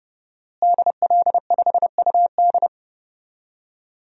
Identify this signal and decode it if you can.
Morse code, heard as a keyed tone, about 31 wpm, 700 Hz. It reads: DL5UD
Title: DXCC-100 Call Signs - 5 Characters - Repeated 2x - 30wpm